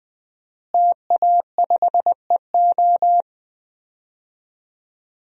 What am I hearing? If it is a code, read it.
Morse code, heard as a keyed tone, about 21 wpm, 700 Hz. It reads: TA5EO
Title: DXCC-100 Call Signs - 5 Characters - Repeated 2x - 20wpm